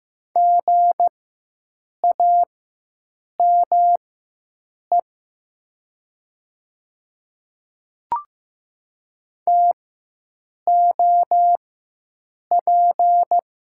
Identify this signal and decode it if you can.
Morse code, heard as a keyed tone, about 15 wpm, spaced as for 7 wpm, 700 Hz. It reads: GAME TOP
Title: 4x Character Spacing - Top 500 Words - 15wpm